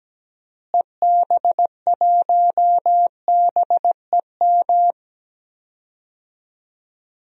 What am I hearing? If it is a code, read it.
Morse code, heard as a keyed tone, about 17 wpm, 700 Hz. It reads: EB1BEM
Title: DXCC-100 Call Signs - 6 Characters - Repeated 3x - 17wpm